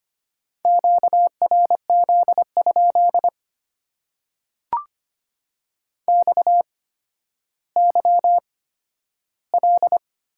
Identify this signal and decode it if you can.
Morse code, heard as a keyed tone, about 25 wpm, spaced as for 12 wpm, 700 Hz. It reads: QRZ? XYL
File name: Top 90 QSO Elements - 8x Character Spacing - 25wpm